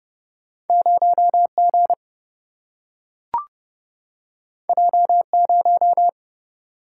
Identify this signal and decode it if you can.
Morse code, heard as a keyed tone, about 30 wpm, 700 Hz. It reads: 0G J0
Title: Single-Letter Single-Number 30wpm